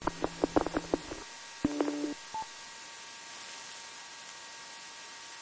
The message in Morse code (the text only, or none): none